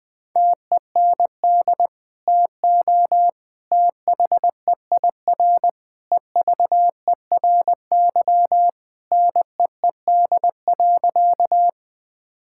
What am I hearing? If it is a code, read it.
Morse code, heard as a keyed tone, about 20 wpm, 700 Hz. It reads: TEND TO THEIR EVERY NEED.
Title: Sentences from Top 1000 Words - Review Part 5 - 20wpm